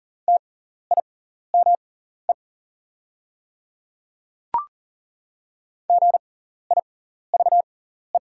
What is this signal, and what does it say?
Morse code, heard as a keyed tone, about 38 wpm, spaced as for 14 wpm, 700 Hz. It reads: TIME GIVE
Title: 6x Character Spacing - Top 500 Words - 40wpm